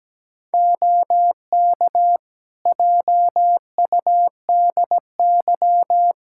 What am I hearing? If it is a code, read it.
Morse code, heard as a keyed tone, about 17 wpm, 700 Hz. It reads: OK JUDY
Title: Practice QSO - Exchange 4 - Ending - 17wpm